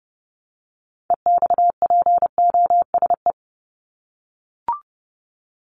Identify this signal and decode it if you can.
Morse code, heard as a keyed tone, about 30 wpm, 700 Hz. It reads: EXPOSE
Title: Top 1501-1750 Words 30wpm